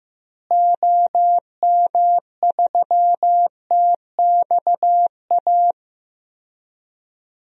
Morse code, heard as a keyed tone, about 15 wpm, 700 Hz. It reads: OM3TXA
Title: DXCC-100 Call Signs - 6 Characters - Repeated 2x - 15wpm